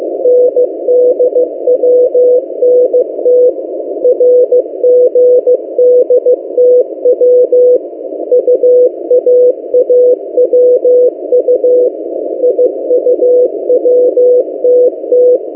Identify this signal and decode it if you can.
Morse code, heard as a keyed tone, about 15 wpm, 500 Hz. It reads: NDWNT RGDTW UAAWU IUWTT
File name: M08a- -1355utc-8096khz